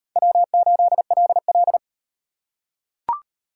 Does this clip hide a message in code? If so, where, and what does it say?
Yes: Morse code from 0.2 to 1.8 s, heard as a keyed tone, about 38 wpm, 700 Hz. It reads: W8LL